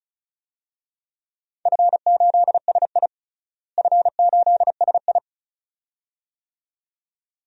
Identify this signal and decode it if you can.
Morse code, heard as a keyed tone, about 34 wpm, 700 Hz. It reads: F8SI F8SI
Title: DXCC-100 Call Signs - 4 Characters - Repeated 2x - 35wpm